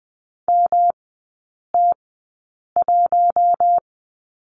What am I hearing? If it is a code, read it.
Morse code, heard as a keyed tone, about 20 wpm, spaced as for 8 wpm, 700 Hz. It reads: MT1
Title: 127 Lesson 21e - Number 6 - Three Characters 20wpm